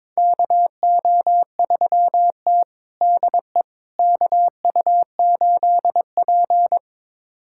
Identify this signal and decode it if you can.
Morse code, heard as a keyed tone, about 22 wpm, 700 Hz. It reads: KO3T DE KU8P
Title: USA Callsign DE Callsign - 4 Characters - 22wpm